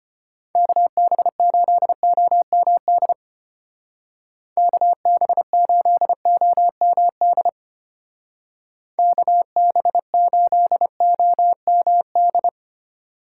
Morse code, heard as a keyed tone, about 31 wpm, 700 Hz. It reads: KB8OMD KB8OMD KB8OMD
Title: Speed Racing - USA 2x3 Call Signs 25wpm